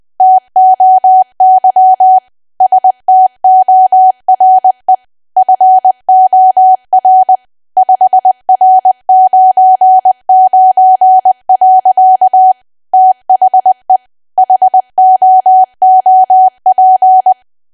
Morse code, heard as a keyed tone, about 20 wpm, 750 Hz. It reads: TOY STORE FOR 5R99. THE HOOP
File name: cwa / W1AW-Level3 / 20WPM